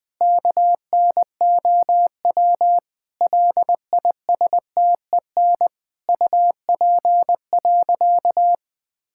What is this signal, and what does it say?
Morse code, heard as a keyed tone, about 20 wpm, 700 Hz. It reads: KNOW LISTEN UP.
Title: Sentences with 7-Word Phrases - 20wpm